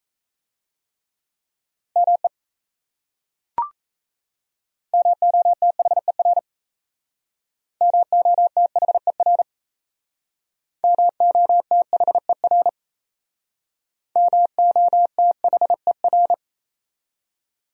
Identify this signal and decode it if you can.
Morse code, heard as a keyed tone, about 36 wpm, 700 Hz. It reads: ME MOTHER MOTHER MOTHER MOTHER